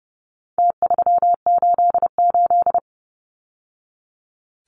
Morse code, heard as a keyed tone, about 30 wpm, 700 Hz. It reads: T388
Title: Numbers - Contest Style - 4-Digit 30wpm